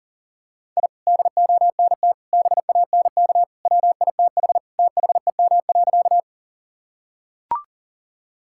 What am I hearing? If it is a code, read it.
Morse code, heard as a keyed tone, about 40 wpm, 700 Hz. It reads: I DONT BANK WITH THEM.
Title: Sentences from Top 1000 Words - Review Part 6 - 40wpm